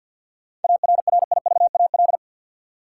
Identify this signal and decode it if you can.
Morse code, heard as a keyed tone, about 53 wpm, 700 Hz. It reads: ARRIVAL
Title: Seven Letter Words - 50wpm